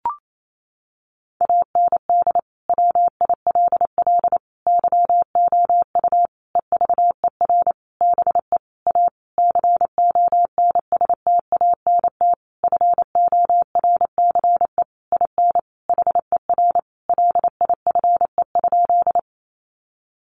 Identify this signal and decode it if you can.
Morse code, heard as a keyed tone, about 28 wpm, 700 Hz. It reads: AND WILL YOU EVER BE A CONSTANT FORCE IN HER LIFE?
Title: Sentences from Top 1500 Words 28wpm